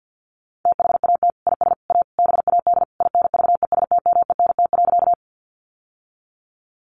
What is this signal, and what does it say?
Morse code, heard as a keyed tone, about 48 wpm, 700 Hz. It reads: THAT IS A BAD INVESTMENT.
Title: Sentences from Top 700 Words 50wpm